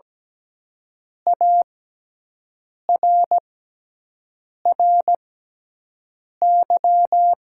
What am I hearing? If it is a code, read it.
Morse code, heard as a keyed tone, about 17 wpm, spaced as for 6 wpm, 700 Hz. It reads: ARRY
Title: 6x Character Spacing - Top 300 Words - 17wpm